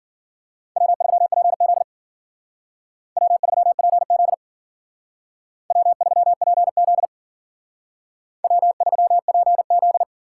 Morse code, heard as a keyed tone, about 49 wpm, 700 Hz. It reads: W3PZ W3PZ W3PZ W3PZ
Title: Speed Racing - USA 1x2 Call Signs 40wpm